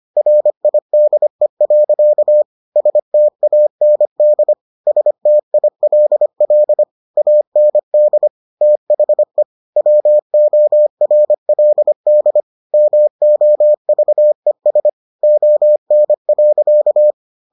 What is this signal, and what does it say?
Morse code, heard as a keyed tone, about 25 wpm, 600 Hz. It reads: RIDE. STAND STILL AND THE WORLD MOVES ON.